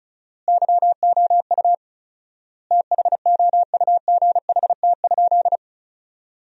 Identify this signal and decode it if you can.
Morse code, heard as a keyed tone, about 34 wpm, 700 Hz. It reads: YOU THOUGHT?
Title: 4x Word-Spacing - Sentences from Top 300 Words - 35wpm